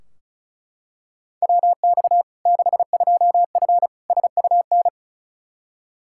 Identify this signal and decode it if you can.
Morse code, heard as a keyed tone, about 36 wpm, 700 Hz. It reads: WX 62F SUN
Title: Practice QSO - Exchange 2 - Info - 35wpm